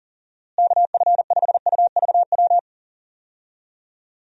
Morse code, heard as a keyed tone, about 40 wpm, 700 Hz. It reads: KF5UVW
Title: USA Call Signs - 6 Characters - Repeated 2x - 40wpm